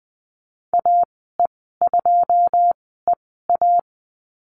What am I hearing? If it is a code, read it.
Morse code, heard as a keyed tone, about 20 wpm, spaced as for 15 wpm, 700 Hz. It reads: AE2EA
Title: 082 Lesson 14g - Number 2 - Call Signs 20wpm